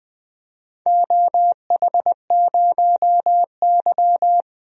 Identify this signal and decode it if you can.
Morse code, heard as a keyed tone, about 21 wpm, 700 Hz. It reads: OH0Y
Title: DXCC-100 Call Signs - 4 Characters - 20wpm